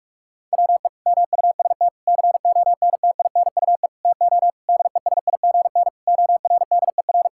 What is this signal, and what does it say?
Morse code, heard as a keyed tone, about 41 wpm, 700 Hz. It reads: WE MUST CONTINUE TO DESIGN ORDER
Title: Sentences from Top 1000 Words - Review Part 6 - 45wpm